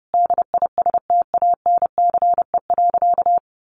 Morse code, heard as a keyed tone, about 30 wpm, 700 Hz. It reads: DISTANCE.
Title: Sentences from Top 1200 Words 30wpm